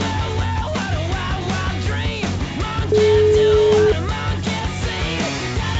Ringback tone (Europe, 425 Hz), with noise.